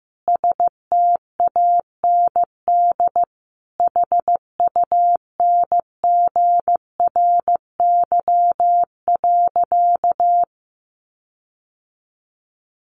Morse code, heard as a keyed tone, about 15 wpm, 700 Hz. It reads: STAND HUNGRY.
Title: Sentences from Top 1400 Words 15wpm